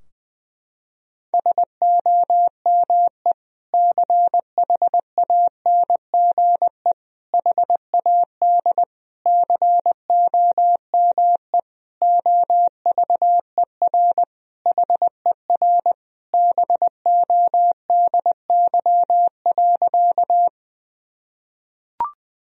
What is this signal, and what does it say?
Morse code, heard as a keyed tone, about 20 wpm, 700 Hz. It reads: SOME CHANGE HAD COME OVER HER BODY.
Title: Sentences from Top 1000 Words - Review Part 2 - 20wpm